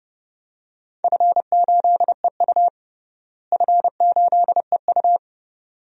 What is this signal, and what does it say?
Morse code, heard as a keyed tone, about 30 wpm, 700 Hz. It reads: F8EU F8EU
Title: DXCC-100 Call Signs - 4 Characters - Repeated 2x - 30wpm